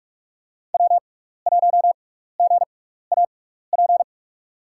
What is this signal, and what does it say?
Morse code, heard as a keyed tone, about 46 wpm, spaced as for 16 wpm, 700 Hz. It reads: W1GAP
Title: USA Call Signs - 5 Characters - 6x Character Spacing - 45wpm